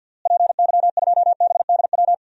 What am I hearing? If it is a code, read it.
Morse code, heard as a keyed tone, about 51 wpm, 700 Hz. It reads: PY2BDW
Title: DXCC-100 Call Signs - 6 Characters - Repeated 3x - 50wpm